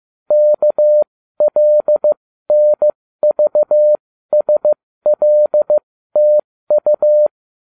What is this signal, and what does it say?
Morse code, heard as a keyed tone, about 15 wpm, spaced as for 13 wpm, 600 Hz. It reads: KLNVSLTU